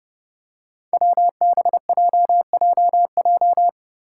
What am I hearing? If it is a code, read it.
Morse code, heard as a keyed tone, about 30 wpm, 700 Hz. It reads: WBJJJ
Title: Five Challenging Letter-Number Combinations 30wpm